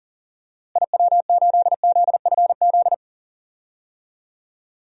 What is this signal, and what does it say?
Morse code, heard as a keyed tone, about 40 wpm, 700 Hz. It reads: IW8ZFZ
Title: DXCC-100 Call Signs - 6 Characters - Repeated 3x - 40wpm